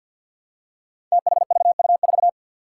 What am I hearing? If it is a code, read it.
Morse code, heard as a keyed tone, about 53 wpm, 700 Hz. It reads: THVU4
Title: Five Challenging Letter-Number Combinations 50wpm